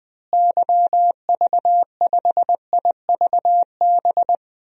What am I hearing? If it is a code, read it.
Morse code, heard as a keyed tone, about 20 wpm, 700 Hz. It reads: YV5IVB
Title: DXCC-100 Call Signs - 6 Characters - Repeated 3x - 20wpm